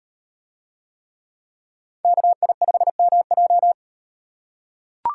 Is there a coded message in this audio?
Yes: Morse code from 2.0 to 3.7 s, heard as a keyed tone, about 38 wpm, 700 Hz. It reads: KI5MJ